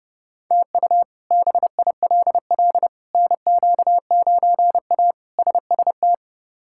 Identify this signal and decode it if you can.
Morse code, heard as a keyed tone, about 32 wpm, 700 Hz. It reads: TU BILL NQ9A SST
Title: K1USN SST QSO Practice - 30wpm